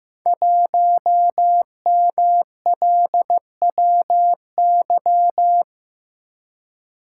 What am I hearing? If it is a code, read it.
Morse code, heard as a keyed tone, about 15 wpm, 700 Hz. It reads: 1MLWY
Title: Code Groups - 5 Letter:Numbers - 15wpm